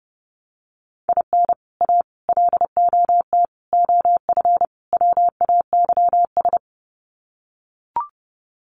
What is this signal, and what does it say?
Morse code, heard as a keyed tone, about 30 wpm, 700 Hz. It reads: IN A LOT OF WAYS
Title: Sets of 5 Words Encore 30wpm